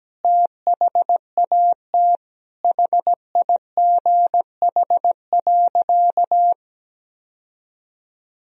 Morse code, heard as a keyed tone, about 17 wpm, 700 Hz. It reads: THAT HIGH.
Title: Sentences from Top 1000 Words - Review Part 1 - 17wpm